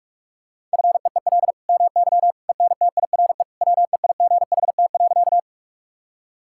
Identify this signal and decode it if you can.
Morse code, heard as a keyed tone, about 45 wpm, 700 Hz. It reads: FEEL MY ENTIRE WEIGHT.
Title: Sentences from Top 1000 Words - Review Part 5 - 45wpm